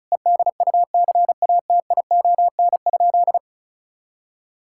Morse code, heard as a keyed tone, about 34 wpm, 700 Hz. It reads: EDUCATION?
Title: Sentences from Top 1000 Words - Review Part 6 - 35wpm